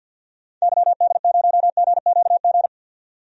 Morse code, heard as a keyed tone, about 46 wpm, 700 Hz. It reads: YD0GQG
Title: DXCC-100 Call Signs - 6 Characters - 50wpm